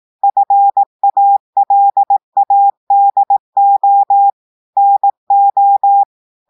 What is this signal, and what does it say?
Morse code, heard as a keyed tone, about 18 wpm, 800 Hz. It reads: FALADO NO
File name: audios / portugais / portugais texte en clair 2 vitesse 18 mots minutes